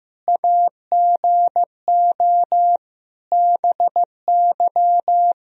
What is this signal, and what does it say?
Morse code, heard as a keyed tone, about 16 wpm, 700 Hz. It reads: AGO BY
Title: Sentences from Top 1000 Words - Review Part 3 - 15wpm